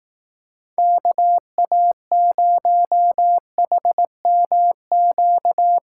Morse code, heard as a keyed tone, about 18 wpm, 700 Hz. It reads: KA0HMQ
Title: Speed Racing - USA 2x3 Call Signs 15wpm